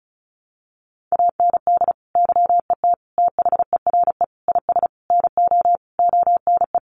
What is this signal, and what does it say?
Morse code, heard as a keyed tone, about 34 wpm, 700 Hz. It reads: AND YET THERE IS NO ONE
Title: Sentences from Top 1000 Words 35wpm